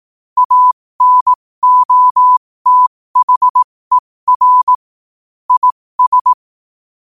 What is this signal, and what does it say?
Morse code, heard as a keyed tone, about 18 wpm, spaced as for 15 wpm, 1000 Hz. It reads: ANOTHER IS
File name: qotdfiles / 15wpm